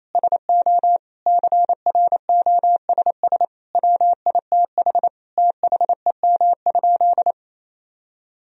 Morse code, heard as a keyed tone, about 28 wpm, 700 Hz. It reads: SO CROSS WITH THEM?
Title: Sentences from Top 1000 Words - Review Part 1 - 28wpm